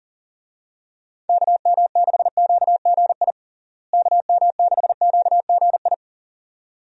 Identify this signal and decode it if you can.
Morse code, heard as a keyed tone, about 37 wpm, 700 Hz. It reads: KM6QGI KM6QGI
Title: USA Call Signs - 6 Characters - Repeated 2x - 40wpm